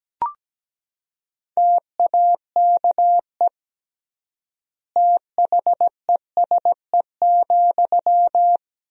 Morse code, heard as a keyed tone, about 17 wpm, 700 Hz. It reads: TAKE THESE,
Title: 3x Word-Spacing - Sentences from Top 300 Words - 17wpm